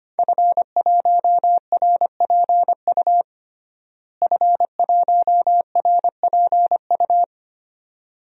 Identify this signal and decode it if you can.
Morse code, heard as a keyed tone, about 25 wpm, 700 Hz. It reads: F1RPU F1RPU
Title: DXCC-100 Call Signs - 5 Characters - Repeated 2x - 25wpm